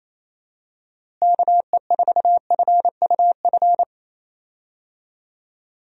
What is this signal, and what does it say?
Morse code, heard as a keyed tone, about 28 wpm, 700 Hz. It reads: KE4FUF
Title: USA Call Signs - 6 Characters - Repeated 2x - 28wpm